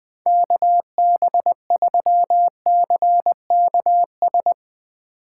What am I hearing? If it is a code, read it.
Morse code, heard as a keyed tone, about 20 wpm, 700 Hz. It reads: KB3CKS
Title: Speed Racing - USA 2x3 Call Signs 20wpm